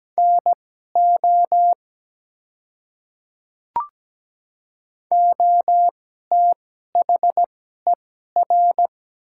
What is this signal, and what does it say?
Morse code, heard as a keyed tone, about 17 wpm, spaced as for 12 wpm, 700 Hz. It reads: NO OTHER
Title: 2x Character Spacing - Top 100 Words - 17wpm